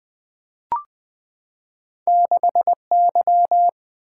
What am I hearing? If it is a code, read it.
Morse code, heard as a keyed tone, about 20 wpm, 700 Hz. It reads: 6Y
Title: Two Letter-Number Combinations - 20wpm